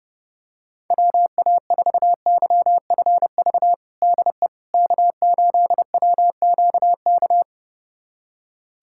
Morse code, heard as a keyed tone, about 30 wpm, 700 Hz. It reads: WA4YFV DE K8WQK
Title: USA Callsign DE Callsign 30wpm